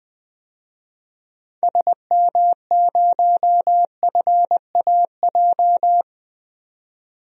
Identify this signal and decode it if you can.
Morse code, heard as a keyed tone, about 20 wpm, 700 Hz. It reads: SM0FAJ
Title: DXCC-100 Call Signs - 6 Characters - 20wpm